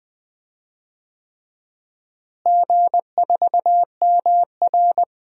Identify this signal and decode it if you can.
Morse code, heard as a keyed tone, about 21 wpm, 700 Hz. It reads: G4MR
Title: DXCC-100 Call Signs - 4 Characters - Repeated 3x - 20wpm